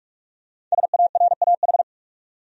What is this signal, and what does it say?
Morse code, heard as a keyed tone, about 46 wpm, 700 Hz. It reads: SARAH
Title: USA Common Names - Max 5 Letters - 45wpm